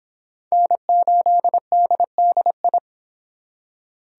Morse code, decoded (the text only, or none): N8DDI